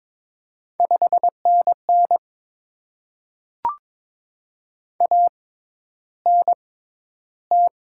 Morse code, heard as a keyed tone, about 22 wpm, spaced as for 11 wpm, 700 Hz. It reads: 5NN ANT
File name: Top 20 QSO Elements - 6x Character Spacing - 22wpm